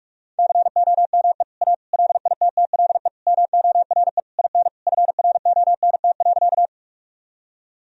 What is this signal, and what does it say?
Morse code, heard as a keyed tone, about 41 wpm, 700 Hz. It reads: COME A LITTLE MORE IN FRONT.
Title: Sentences from Top 1000 Words - Review Part 2 - 45wpm